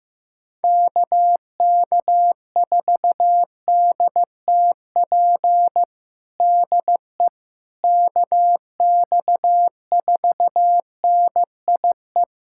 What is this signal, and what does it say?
Morse code, heard as a keyed tone, about 15 wpm, 700 Hz. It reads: KK4DTP DE KX4NIE
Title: USA Callsign DE Callsign - 6 Characters - 15wpm